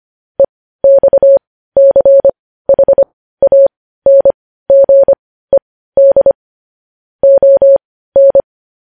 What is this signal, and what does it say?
Morse code, heard as a keyed tone, about 25 wpm, spaced as for 15 wpm, 550 Hz. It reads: EXCHANGED ON
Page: News Headlines in Morse Code at 15 WPM